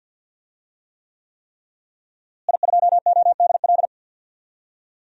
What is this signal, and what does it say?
Morse code, heard as a keyed tone, about 51 wpm, 700 Hz. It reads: I2ODL
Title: DXCC-100 Call Signs - 5 Characters - Repeated 3x - 50wpm